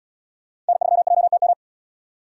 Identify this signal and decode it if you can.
Morse code, heard as a keyed tone, about 77 wpm, 690 Hz. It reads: N3JTM